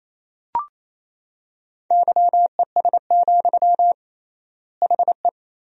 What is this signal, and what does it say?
Morse code, heard as a keyed tone, about 28 wpm, 700 Hz. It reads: YES, HE
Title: 3x Word-Spacing - Sentences from Top 500 Words - 28wpm